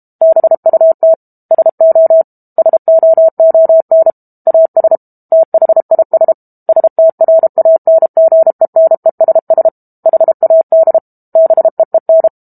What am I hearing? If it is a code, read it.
Morse code, heard as a keyed tone, about 32 wpm, 650 Hz. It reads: BUT SO SOON AS THIS STRANGENESS HAD BEEN
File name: Book Two / 06-Chapter 6 - 32 WPM